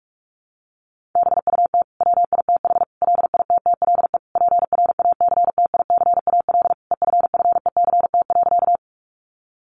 Morse code, heard as a keyed tone, about 44 wpm, 700 Hz. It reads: BUT WITH LITTLE PRACTICAL EFFECT.
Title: Sentences from Top 1400 Words 45wpm